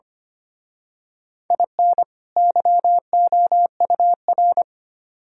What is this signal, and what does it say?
Morse code, heard as a keyed tone, about 29 wpm, 700 Hz. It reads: IN YOUR
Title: Speed-Racing - Sets of 2 Words 25wpm